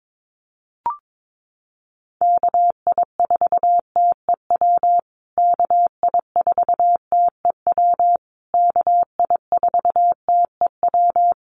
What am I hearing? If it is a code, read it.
Morse code, heard as a keyed tone, about 22 wpm, 700 Hz. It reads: KI4TEW KI4TEW KI4TEW
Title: USA Call Signs Repeated 3x 22wpm